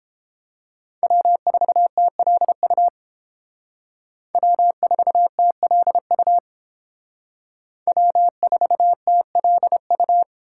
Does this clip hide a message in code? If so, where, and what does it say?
Yes: Morse code from 1.0 to 10.2 s, heard as a keyed tone, about 28 wpm, 700 Hz. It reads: W4TLU W4TLU W4TLU